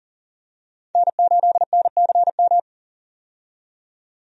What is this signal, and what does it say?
Morse code, heard as a keyed tone, about 40 wpm, 700 Hz. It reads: N8NCM
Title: USA Call Signs - 5 Characters - 40wpm